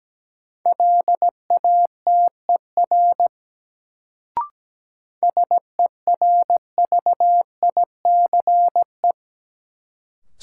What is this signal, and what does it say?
Morse code, heard as a keyed tone, about 17 wpm, 700 Hz. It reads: LATER SERVICE
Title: Sentences from Top 700 Words - New Word Drill - 17wpm